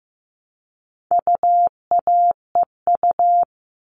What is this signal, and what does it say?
Morse code, heard as a keyed tone, about 15 wpm, 700 Hz. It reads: UAEU
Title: All Cut Numbers - 4-Digit 15wpm